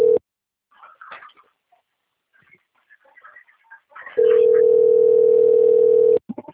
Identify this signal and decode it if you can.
Ringback tone (North America), 440 + 480 Hz.